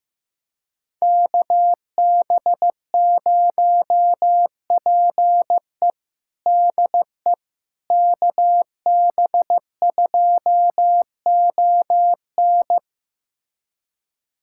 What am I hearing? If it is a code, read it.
Morse code, heard as a keyed tone, about 16 wpm, 700 Hz. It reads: KB0PE DE KB2ON
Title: USA Callsign DE Callsign - 5 Characters - 15wpm